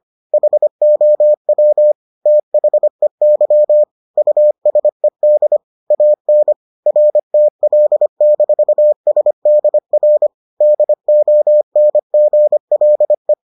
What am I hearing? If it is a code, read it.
Morse code, heard as a keyed tone, about 25 wpm, 600 Hz. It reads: HOW THEY USED AN RTL-SDR DONGLE